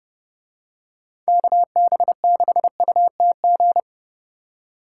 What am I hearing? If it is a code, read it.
Morse code, heard as a keyed tone, about 29 wpm, 700 Hz. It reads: KB6UTG